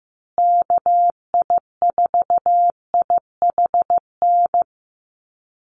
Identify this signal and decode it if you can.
Morse code, heard as a keyed tone, about 15 wpm, 700 Hz. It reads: KI4IHN
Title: USA Call Signs Repeated 3x 15wpm